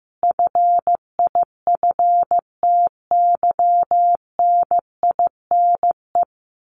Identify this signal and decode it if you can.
Morse code, heard as a keyed tone, about 15 wpm, 700 Hz. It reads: FIFTYNINE
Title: A Sound Of Thunder 15wpm